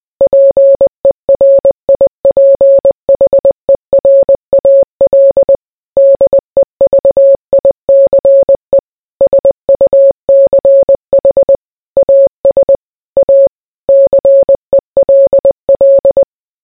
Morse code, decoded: PERIPHERAL DEVICE SUCH AS A CELL